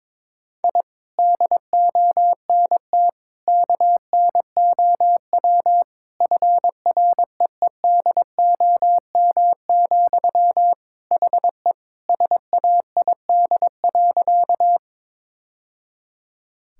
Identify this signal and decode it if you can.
Morse code, heard as a keyed tone, about 22 wpm, 700 Hz. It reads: I DONT KNOW FREEDOM, HE SAID.
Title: Sentences from Top 1000 Words - Review Part 5 - 22wpm